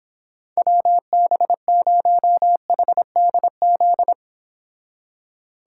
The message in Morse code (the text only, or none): WB0HDZ